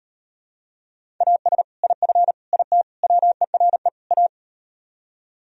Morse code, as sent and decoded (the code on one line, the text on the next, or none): .- ... / .. ..-. / .. - / .-- . .-. / .-
AS IF IT WERE A